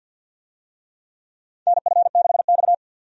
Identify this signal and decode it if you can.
Morse code, heard as a keyed tone, about 52 wpm, 700 Hz. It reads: NF6X
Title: USA Call Signs - 4 Characters - 50wpm